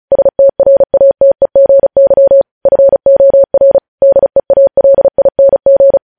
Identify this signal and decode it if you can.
Morse code, heard as a keyed tone, about 36 wpm, 550 Hz. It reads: STRATEGY FOR DEALING
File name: podcast / News headlines-1x-35x35